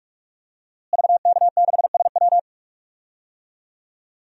Morse code, heard as a keyed tone, about 47 wpm, 700 Hz. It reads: VK6SW
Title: DXCC-100 Call Signs - 5 Characters - Repeated 2x - 45wpm